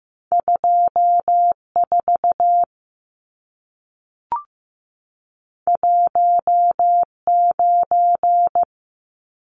Morse code, heard as a keyed tone, about 15 wpm, 700 Hz. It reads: 24 19
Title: Numbers - 2-Digit 15wpm